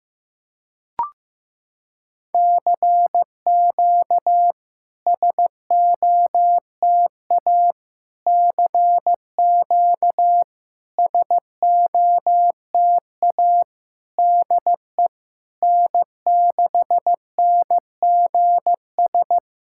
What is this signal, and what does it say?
Morse code, heard as a keyed tone, about 15 wpm, 700 Hz. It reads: CQ SOTA CQ SOTA DE N6NGS
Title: SOTA QSO Practice 15wpm